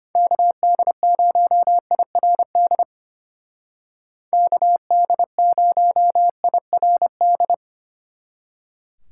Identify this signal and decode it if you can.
Morse code, heard as a keyed tone, about 31 wpm, 700 Hz. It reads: KD0IRD KD0IRD
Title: Speed Racing - USA 2x3 Call Signs 25wpm